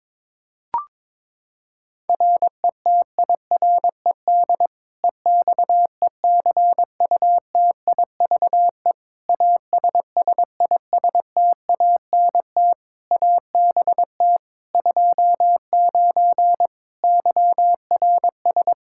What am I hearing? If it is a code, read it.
Morse code, heard as a keyed tone, about 22 wpm, 700 Hz. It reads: RETIRED EXECUTIVE ASSISTANT ABT 29 YRS
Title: Practice QSO - Exchange 3 - More Info - 22wpm